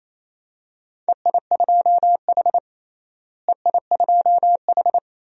Morse code, heard as a keyed tone, about 28 wpm, 700 Hz. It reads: EI2H EI2H
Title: DXCC-100 Call Signs - 4 Characters - Repeated 3x - 28wpm